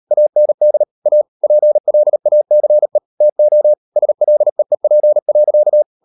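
Morse code, heard as a keyed tone, about 38 wpm, 600 Hz. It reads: AND A PLACE TO SLEEP.
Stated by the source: Morse code practice at 38 WPM, sending Ken Follett's novel Never in English, from CW med Gnister.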